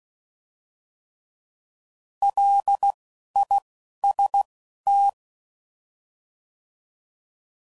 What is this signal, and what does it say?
Morse code, heard as a keyed tone, about 16 wpm, spaced as for 11 wpm, 800 Hz. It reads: LIST